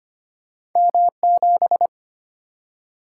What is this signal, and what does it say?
Morse code, heard as a keyed tone, about 25 wpm, 700 Hz. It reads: M7
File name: Two Letter-Number Combinations - 25wpm